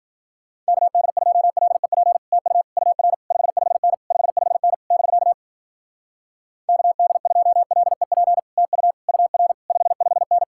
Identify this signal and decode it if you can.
Morse code, heard as a keyed tone, about 51 wpm, 700 Hz. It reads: KD2LEP TU UR 55N 55N <BK> KD2LEP TU UR 55N